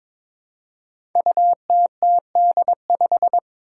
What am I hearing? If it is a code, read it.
Morse code, heard as a keyed tone, about 22 wpm, 700 Hz. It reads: UTTD5
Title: Five Challenging Letter-Number Combinations 22wpm